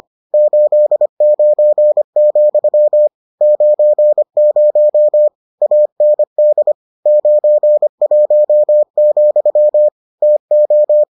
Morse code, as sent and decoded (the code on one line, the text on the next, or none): ---.. ----. --..-- / ----. ----- / .- -. -.. / ----. .---- --..-- / - ---
89, 90 AND 91, TO